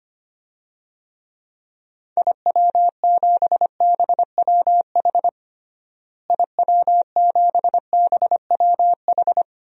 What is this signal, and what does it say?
Morse code, heard as a keyed tone, about 25 wpm, 700 Hz. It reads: IW7BWH IW7BWH
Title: DXCC-100 Call Signs - 6 Characters - Repeated 2x - 25wpm